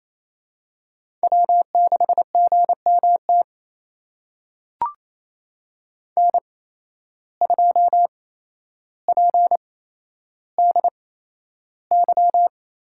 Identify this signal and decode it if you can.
Morse code, heard as a keyed tone, about 27 wpm, spaced as for 12 wpm, 700 Hz. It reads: W6GMT N2PDY
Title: USA Call Signs - 5 Characters - 8x Character Spacing - 28wpm